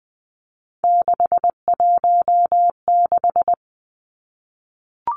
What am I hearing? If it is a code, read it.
Morse code, heard as a keyed tone, about 20 wpm, 700 Hz. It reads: 616
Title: Three Challenging Letter-Number Combinations 20wpm